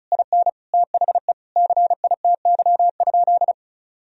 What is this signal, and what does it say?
Morse code, heard as a keyed tone, about 34 wpm, 700 Hz. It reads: IN THE CITY?
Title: Sentences from Top 1000 Words - Review Part 1 - 35wpm